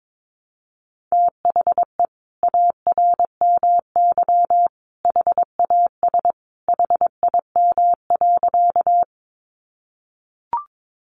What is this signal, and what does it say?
Morse code, heard as a keyed tone, about 22 wpm, 700 Hz. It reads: THE ARMY HAS HIM.
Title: Sentences from Top 800 Words 22wpm